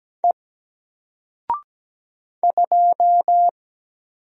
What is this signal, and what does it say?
Morse code, heard as a keyed tone, about 17 wpm, 700 Hz. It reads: E 2
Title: Single Letter-Number 17wpm